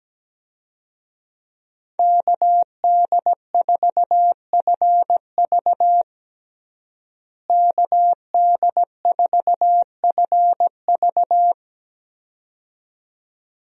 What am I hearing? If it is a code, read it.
Morse code, heard as a keyed tone, about 17 wpm, 700 Hz. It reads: KD4FV KD4FV
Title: USA Call Signs - 5 Characters - Repeated 2x - 17wpm